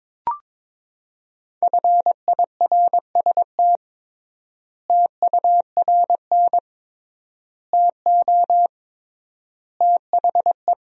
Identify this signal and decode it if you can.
Morse code, heard as a keyed tone, about 22 wpm, 700 Hz. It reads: FIRST TURN TO THE
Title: 3x Word-Spacing - Sentences from Top 300 Words - 22wpm